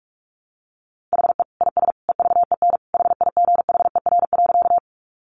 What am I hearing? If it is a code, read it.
Morse code, heard as a keyed tone, about 44 wpm, 700 Hz. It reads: HE IS EVEN HIGHER.
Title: Additional Easy Phrases 45wpm